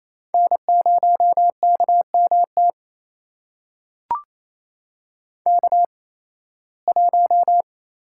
Morse code, heard as a keyed tone, about 28 wpm, spaced as for 18 wpm, 700 Hz. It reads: N0KMT K1
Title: USA Call Signs - 5 Characters - 8x Character Spacing - 28wpm